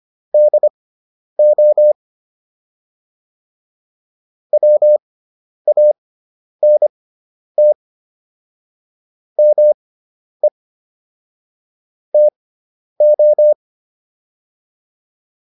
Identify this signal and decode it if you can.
Morse code, heard as a keyed tone, about 25 wpm, spaced as for 10 wpm, 600 Hz. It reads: DO WANT ME TO
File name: WD401 10